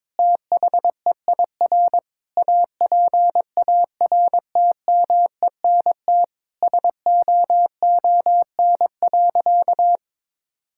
Morse code, heard as a keyed tone, about 21 wpm, 700 Hz. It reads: THEIR APARTMENT SOON.
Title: Sentences from Top 1000 Words - Review Part 6 - 22wpm